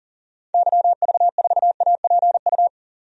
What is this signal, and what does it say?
Morse code, heard as a keyed tone, about 40 wpm, 700 Hz. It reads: YV4APU
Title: DXCC-100 Call Signs - 6 Characters - 40wpm